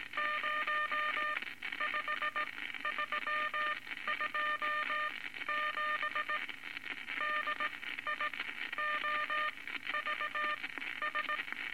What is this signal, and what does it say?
Morse code, heard as a keyed tone, about 18 wpm, spaced as for 13 wpm, 1210 Hz, with noise. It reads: 05327 DIOVS
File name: Morse Code Records / 19 A LCM 166 Ten Words Per Minute